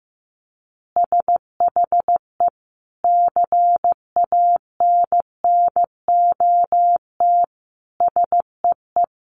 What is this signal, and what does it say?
Morse code, heard as a keyed tone, about 15 wpm, 700 Hz. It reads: SHE CANNOT SEE
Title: Additional Easy Phrases 15wpm